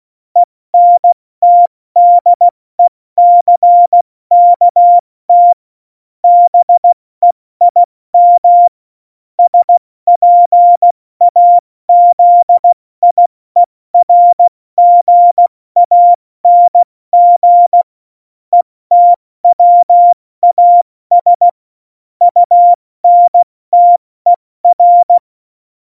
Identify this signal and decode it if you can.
Morse code, heard as a keyed tone, about 16 wpm, 700 Hz. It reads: ENTDECKT BEIM SPAZIERGANG ETWAS UNTER
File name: Morsekurs / Witze / Witz 28 Juli 2020, 70 F80